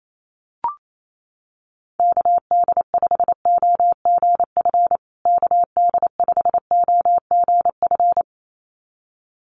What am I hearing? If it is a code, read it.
Morse code, heard as a keyed tone, about 28 wpm, 700 Hz. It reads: KD5OGF KD5OGF
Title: USA Call Signs Repeated 2x 28wpm